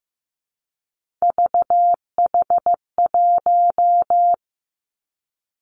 Morse code, heard as a keyed tone, about 15 wpm, 700 Hz. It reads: VH1
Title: Three Challenging Letter-Number Combinations 15wpm